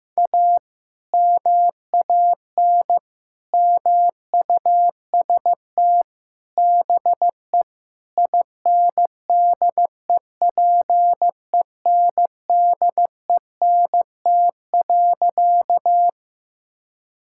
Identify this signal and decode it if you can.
Morse code, heard as a keyed tone, about 16 wpm, 700 Hz. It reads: A MAN MUST BE INDEPENDENT.
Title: Sentences from Top 1000 Words - Review Part 6 - 15wpm